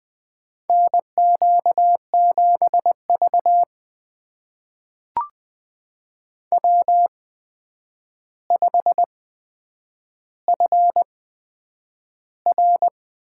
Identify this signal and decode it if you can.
Morse code, heard as a keyed tone, about 20 wpm, spaced as for 9 wpm, 700 Hz. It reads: NQ7V W5FR
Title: USA Call Signs - 4 Characters - 8x Character Spacing - 20wpm